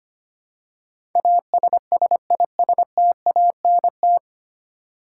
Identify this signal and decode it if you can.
Morse code, heard as a keyed tone, about 25 wpm, 700 Hz. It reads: ASSISTANT
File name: Nine Letter Words - 25wpm